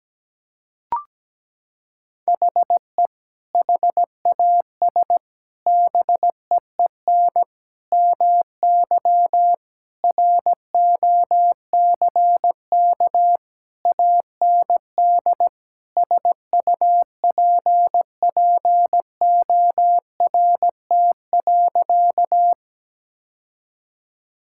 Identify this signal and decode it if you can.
Morse code, heard as a keyed tone, about 17 wpm, 700 Hz. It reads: HE HAS BEEN MY ROCK AND SUPPORT.
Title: Sentences from Top 1000 Words - Review Part 5 - 17wpm